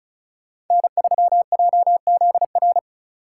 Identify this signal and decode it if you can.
Morse code, heard as a keyed tone, about 33 wpm, 700 Hz. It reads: N3JZR
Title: Speed Racing - USA 1x3 Call Signs 35wpm